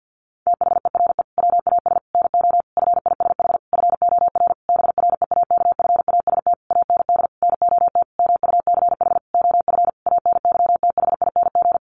Morse code, heard as a keyed tone, about 50 wpm, 700 Hz. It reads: THERE WAS NO FISH FOR BREAKFAST AND NOT MUCH OF ANYTHING
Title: Sentences from Top 1500 Words 50wpm